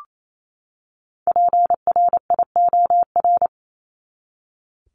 Morse code, heard as a keyed tone, about 28 wpm, 700 Hz. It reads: PRIOR
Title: Top 1501-1750 Words 28wpm